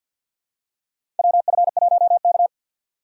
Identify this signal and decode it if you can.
Morse code, heard as a keyed tone, about 51 wpm, 700 Hz. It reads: WF1K
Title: USA Call Signs - 4 Characters - 3x Character Spacing - 50wpm